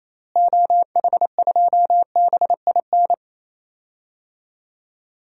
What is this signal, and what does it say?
Morse code, heard as a keyed tone, about 28 wpm, 700 Hz. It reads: OH2BIN
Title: DXCC-100 Call Signs - 6 Characters - Repeated 3x - 28wpm